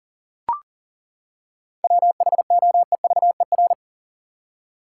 Morse code, heard as a keyed tone, about 40 wpm, 700 Hz. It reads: WHOEVER
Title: Seven Letter Words - 40wpm